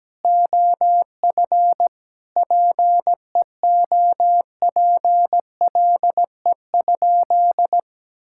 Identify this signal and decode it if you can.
Morse code, heard as a keyed tone, about 17 wpm, 700 Hz. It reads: OF PEOPLE?
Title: Sentences from Top 1000 Words - Review Part 6 - 17wpm